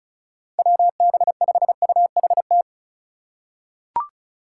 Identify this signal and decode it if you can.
Morse code, heard as a keyed tone, about 34 wpm, 700 Hz. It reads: WB5UHT
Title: USA Call Signs - 6 Characters - 2x Character Spacing - 35wpm